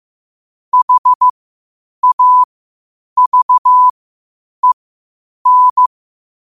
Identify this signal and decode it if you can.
Morse code, heard as a keyed tone, about 15 wpm, spaced as for 8 wpm, 1000 Hz. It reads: HAVEN
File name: qotdfiles / 8wpm